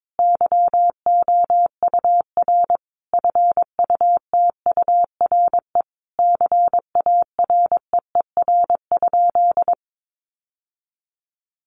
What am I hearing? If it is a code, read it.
Morse code, heard as a keyed tone, about 22 wpm, 700 Hz. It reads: YOUR FUTURE CAREER?
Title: Sentences from Top 1000 Words 22wpm